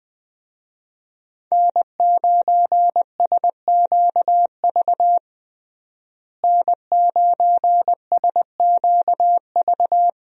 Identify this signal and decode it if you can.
Morse code, heard as a keyed tone, about 20 wpm, 700 Hz. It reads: N9SQV N9SQV
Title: USA Call Signs - 5 Characters - Repeated 2x - 20wpm